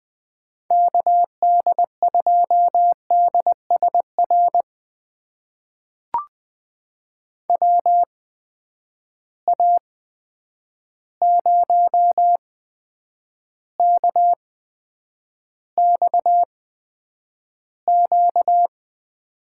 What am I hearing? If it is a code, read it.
Morse code, heard as a keyed tone, about 20 wpm, spaced as for 9 wpm, 700 Hz. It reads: KD2DSR WA0KXQ
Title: USA Call Signs - 6 Characters - 8x Character Spacing - 20wpm